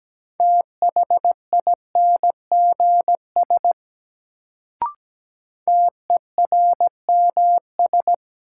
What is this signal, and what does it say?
Morse code, heard as a keyed tone, about 17 wpm, 700 Hz. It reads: THINGS TERMS
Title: Sets of 3 Words - New Word Drill - 17wpm